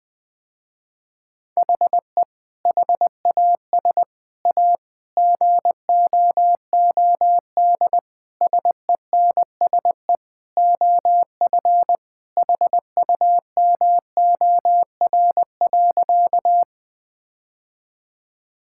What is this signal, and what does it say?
Morse code, heard as a keyed tone, about 20 wpm, 700 Hz. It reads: HE HAS A GOOD SENSE OF HUMOR.
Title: Sentences with 2-Word Phrases - Spoken Beginning - 20wpm